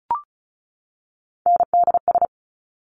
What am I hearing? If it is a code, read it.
Morse code, heard as a keyed tone, about 36 wpm, 700 Hz. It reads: NDS
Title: Top Three Letter Combinations in Words 35wpm